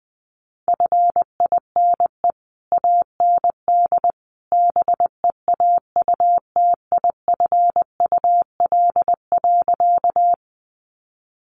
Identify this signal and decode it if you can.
Morse code, heard as a keyed tone, about 20 wpm, 700 Hz. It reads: FINE AND BEAUTIFUL.
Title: Sentences from Top 900 Words 20wpm